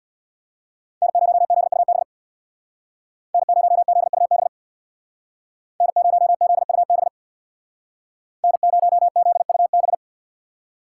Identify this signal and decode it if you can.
Morse code, heard as a keyed tone, about 64 wpm, 700 Hz. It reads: N0ZUB N0ZUB N0ZUB N0ZUB